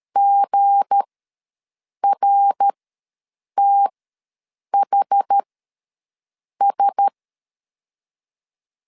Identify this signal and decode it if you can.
Morse code, heard as a keyed tone, about 13 wpm, spaced as for 6 wpm, 800 Hz. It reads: GRTHS
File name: LEZIONI / 7 GF9Z / 7 GF9 ws 12wpm 3344